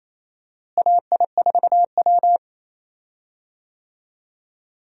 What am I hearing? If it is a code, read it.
Morse code, heard as a keyed tone, about 28 wpm, 700 Hz. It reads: AI4W
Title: USA Call Signs - 4 Characters - Repeated 2x - 28wpm